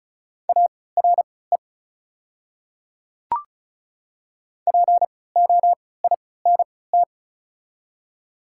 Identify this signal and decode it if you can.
Morse code, heard as a keyed tone, about 36 wpm, spaced as for 20 wpm, 700 Hz. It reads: ARE POINT
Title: 3x Character Spacing - Top 200 Words - 35wpm